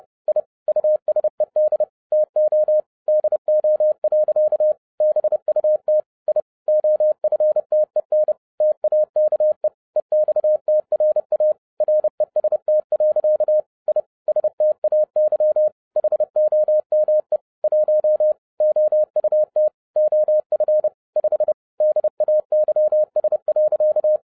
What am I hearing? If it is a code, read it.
Morse code, heard as a keyed tone, about 30 wpm, 610 Hz. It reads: I USED TO DO. BUT I OFTEN TAKE EXTRA REST. I STAY HOME 1 OUT OF 5 DAYS.
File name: 2011-07-mp3 / n60 BACK FROM A BIG ACCIDENT 30wpm